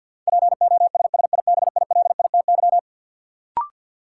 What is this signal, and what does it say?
Morse code, heard as a keyed tone, about 52 wpm, 700 Hz. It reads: POSSIBILITY
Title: Sets of 3 Words - New Word Drill - 50wpm